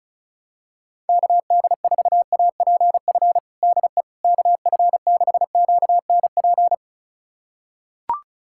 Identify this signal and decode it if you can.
Morse code, heard as a keyed tone, about 34 wpm, 700 Hz. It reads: KD4APF DE KF6QNP
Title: USA Callsign DE Callsign - 6 Characters - 35wpm